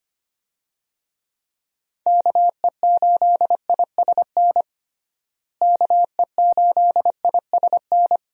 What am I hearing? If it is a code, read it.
Morse code, heard as a keyed tone, about 25 wpm, 700 Hz. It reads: KE8ISN KE8ISN
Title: USA Call Signs - 6 Characters - Repeated 2x - 25wpm